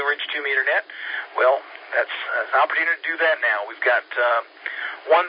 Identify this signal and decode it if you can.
Emergency Alert System (SAME) tones, heard as harsh buzzing bursts, with speech. It is heard from 0.8 to 5.0 s.